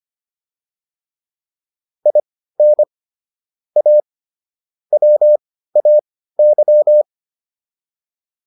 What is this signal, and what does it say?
Morse code, heard as a keyed tone, about 25 wpm, spaced as for 15 wpm, 600 Hz. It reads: IN A WAY